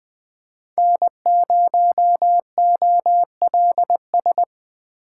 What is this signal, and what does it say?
Morse code, heard as a keyed tone, about 21 wpm, 700 Hz. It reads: N0OLS
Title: USA Call Signs - 4x Character Spacing - 20wpm